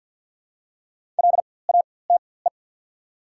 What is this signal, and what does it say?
Morse code, heard as a keyed tone, about 48 wpm, spaced as for 23 wpm, 700 Hz. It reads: LATE
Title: 4x Character Spacing - Top 300 Words - 50wpm